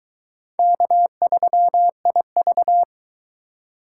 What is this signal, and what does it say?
Morse code, heard as a keyed tone, about 22 wpm, 700 Hz. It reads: K3IV